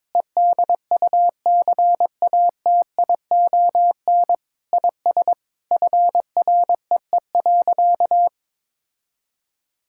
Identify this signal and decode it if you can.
Morse code, heard as a keyed tone, about 22 wpm, 700 Hz. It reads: EDUCATION IS FREE.
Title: Sentences from Top 1000 Words - Review Part 6 - 22wpm